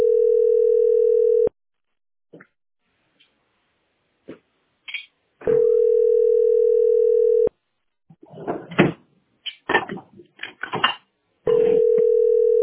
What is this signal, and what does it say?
Ringback tone (North America), 440 + 480 Hz.